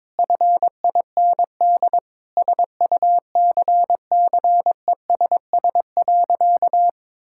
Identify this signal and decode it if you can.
Morse code, heard as a keyed tone, about 22 wpm, 700 Hz. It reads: FIND SUCCESS.
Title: Sentences from Top 1000 Words - Review Part 5 - 22wpm